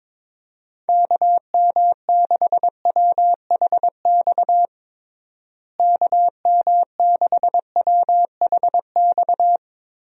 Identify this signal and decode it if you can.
Morse code, heard as a keyed tone, about 22 wpm, 700 Hz. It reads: KM6WHX KM6WHX
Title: USA Call Signs - 6 Characters - Repeated 3x - 22wpm